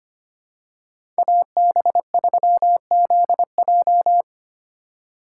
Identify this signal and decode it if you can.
Morse code, heard as a keyed tone, about 25 wpm, 700 Hz. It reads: AB3ZJ
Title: USA Call Signs - 5 Characters - 3x Character Spacing - 25wpm